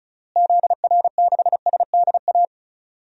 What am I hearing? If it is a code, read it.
Morse code, heard as a keyed tone, about 34 wpm, 700 Hz. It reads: ZR6SDA
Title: DXCC-100 Call Signs - 6 Characters - Repeated 3x - 35wpm